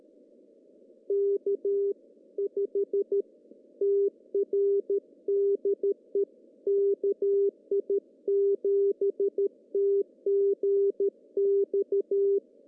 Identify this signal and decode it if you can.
Morse code, heard as a keyed tone, about 13 wpm, 400 Hz. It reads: K5TR DE KI7TGX
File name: CW / cqww-cw-k5tr